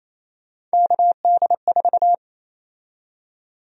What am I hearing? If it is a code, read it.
Morse code, heard as a keyed tone, about 28 wpm, 700 Hz. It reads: KD4
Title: USA Call Sign Prefixes - 28wpm